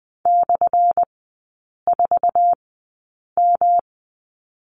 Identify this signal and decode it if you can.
Morse code, heard as a keyed tone, about 20 wpm, spaced as for 8 wpm, 700 Hz. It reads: /4M
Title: 178 Lesson 29e - Slash - Three Characters 20wpm